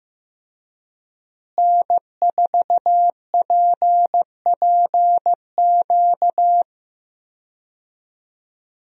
Morse code, heard as a keyed tone, about 15 wpm, 700 Hz. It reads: N4PPQ
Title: USA Call Signs - 5 Characters - Repeated 3x - 15wpm